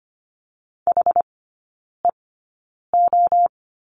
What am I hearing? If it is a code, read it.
Morse code, heard as a keyed tone, about 25 wpm, spaced as for 9 wpm, 700 Hz. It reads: HEO
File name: Lesson 14e - Number 2 - Three Characters 25wpm